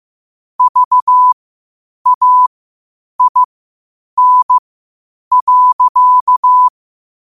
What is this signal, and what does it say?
Morse code, heard as a keyed tone, about 15 wpm, spaced as for 8 wpm, 1000 Hz. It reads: VAIN.
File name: qotdfiles / 8wpm